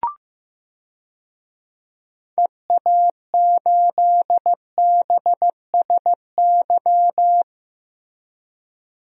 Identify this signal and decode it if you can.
Morse code, heard as a keyed tone, about 15 wpm, 700 Hz. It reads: EA8BSY
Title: DXCC-100 Call Signs - 6 Characters - Repeated 2x - 15wpm